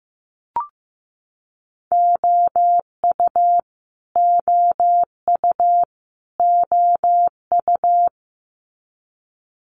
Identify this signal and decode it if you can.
Morse code, heard as a keyed tone, about 15 wpm, 700 Hz. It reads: OU OU OU
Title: Multi-letter Phonemes Repeated 3x 15wpm